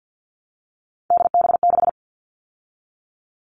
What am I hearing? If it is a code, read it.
Morse code, heard as a keyed tone, about 48 wpm, 700 Hz. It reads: DB6
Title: All Cut Numbers - 3-Digit 50wpm